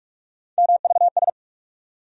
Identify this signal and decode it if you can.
Morse code, heard as a keyed tone, about 46 wpm, 700 Hz. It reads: MVS
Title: USA Call Sign Postfixes - 3 Letters 45wpm